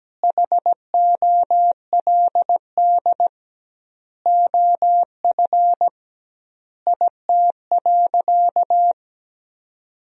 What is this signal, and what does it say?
Morse code, heard as a keyed tone, about 17 wpm, 700 Hz. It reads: HOLD OF IT.
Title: 2x Word-Spacing - Sentences from Top 400 Words - 17wpm